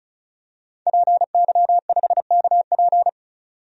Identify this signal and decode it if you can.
Morse code, heard as a keyed tone, about 34 wpm, 700 Hz. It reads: PY5KP
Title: DXCC-100 Call Signs - 5 Characters - 35wpm